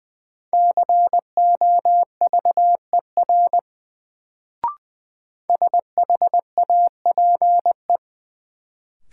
Morse code, heard as a keyed tone, about 20 wpm, 700 Hz. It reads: COVER SHAPE
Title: Sentences from Top 500 Words - New Word Drill - 20wpm